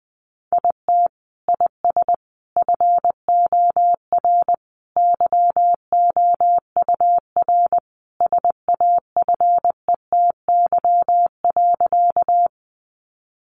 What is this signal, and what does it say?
Morse code, heard as a keyed tone, about 20 wpm, 700 Hz. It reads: IT IS FOR YOUR SAFETY.
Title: Sentences from Top 700 Words 20wpm